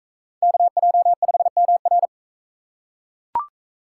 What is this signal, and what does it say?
Morse code, heard as a keyed tone, about 40 wpm, 700 Hz. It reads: KJ5MR